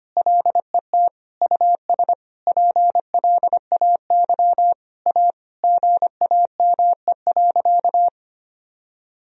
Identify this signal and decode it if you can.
Morse code, heard as a keyed tone, about 25 wpm, 690 Hz. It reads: LET US PLAY A GAME.
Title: Sentences from Top 1000 Words - Review Part 4 - 25wpm